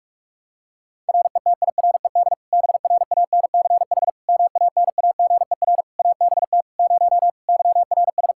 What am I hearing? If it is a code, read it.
Morse code, heard as a keyed tone, about 46 wpm, 700 Hz. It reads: RETIRED BRANCH MANAGER ABT 0 YRS
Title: Practice QSO - Exchange 3 - More Info - 45wpm